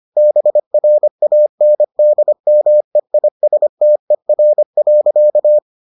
Morse code, heard as a keyed tone, about 25 wpm, 600 Hz. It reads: BRANDMEISTER.